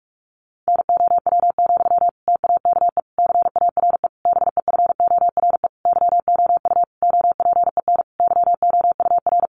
Morse code, heard as a keyed tone, about 44 wpm, 700 Hz. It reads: NOW, TAKE CARE BEFORE YOU OPEN YOUR
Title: Sentences from Top 1500 Words 45wpm